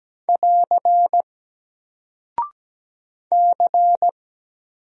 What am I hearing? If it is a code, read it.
Morse code, heard as a keyed tone, about 17 wpm, 700 Hz. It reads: <AR> C
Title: Single Character (Letter:Number:Punctuation:Prosign) 17wpm